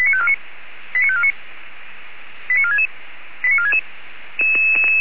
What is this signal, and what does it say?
Five-tone selective call (ZVEI), heard as quick runs of five tones, tones of about 70 ms. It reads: 89549 89466 89466
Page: audio.f85360.feuerwehr-orlamuende.org